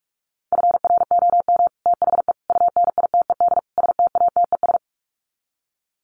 Morse code, heard as a keyed tone, about 44 wpm, 700 Hz. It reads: FROM THE UNITED STATES
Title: Sets of 4 Words 45wpm